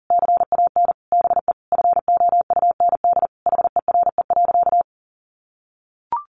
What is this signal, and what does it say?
Morse code, heard as a keyed tone, about 40 wpm, 700 Hz. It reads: CAN BE FOUND HERE.
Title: Sentences from Top 1500 Words 40wpm